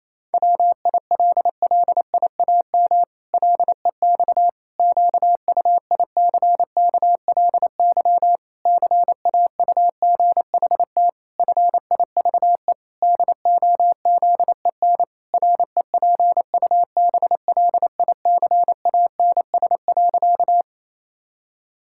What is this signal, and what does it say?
Morse code, heard as a keyed tone, about 28 wpm, 700 Hz. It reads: WILLIAM LEX QUICKLY CAUGHT FIVE DOZEN REPUBLICANS.
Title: Pangram Challenge v2 - 28wpm